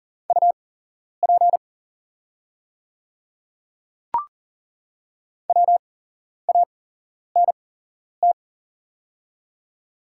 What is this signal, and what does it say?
Morse code, heard as a keyed tone, about 40 wpm, spaced as for 11 wpm, 700 Hz. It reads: UP WANT
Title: 8x Character Spacing - Top 200 Words - 40wpm